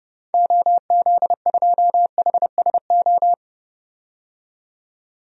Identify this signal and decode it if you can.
Morse code, heard as a keyed tone, about 30 wpm, 700 Hz. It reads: OZ2HSO
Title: DXCC-100 Call Signs - 6 Characters - 30wpm